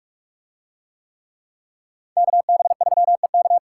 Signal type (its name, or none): Morse code